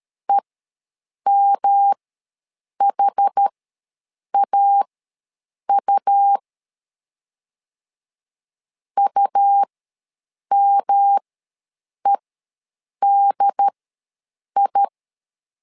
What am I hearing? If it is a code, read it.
Morse code, heard as a keyed tone, about 13 wpm, spaced as for 7 wpm, 800 Hz. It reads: EMHAU UMEDI
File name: LEZIONI / 3 MWRO / 3 M ws 12wpm 1745